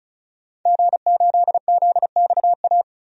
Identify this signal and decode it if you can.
Morse code, heard as a keyed tone, about 36 wpm, 700 Hz. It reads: G8ZXA
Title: DXCC-100 Call Signs - 5 Characters - Repeated 3x - 35wpm